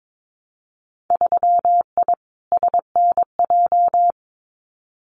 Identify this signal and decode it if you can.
Morse code, heard as a keyed tone, about 22 wpm, 700 Hz. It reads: 3I SNJ
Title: Winter Field Day Exchanges Encore 22wpm